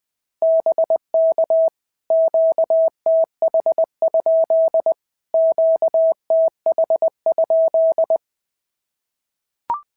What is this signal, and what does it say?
Morse code, heard as a keyed tone, about 18 wpm, 650 Hz. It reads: BK QTH? QTH?
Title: POTA QSO Practice - 20wpm